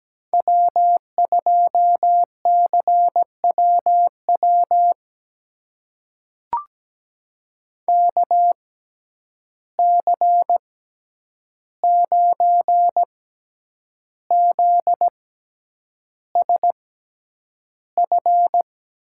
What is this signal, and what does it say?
Morse code, heard as a keyed tone, about 17 wpm, spaced as for 8 wpm, 700 Hz. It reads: W2CWW KC9ZSF
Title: USA Call Signs - 6x Character Spacing - 17wpm